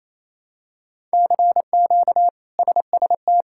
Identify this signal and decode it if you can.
Morse code, heard as a keyed tone, about 28 wpm, 700 Hz. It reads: CQ SST
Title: K1USN SST QSO Practice - 28wpm